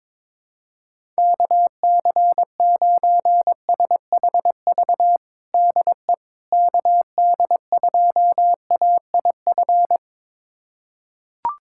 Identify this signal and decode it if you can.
Morse code, heard as a keyed tone, about 22 wpm, 700 Hz. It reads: KC9SHV DE KD2AIF
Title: USA Callsign DE Callsign - 6 Characters - 22wpm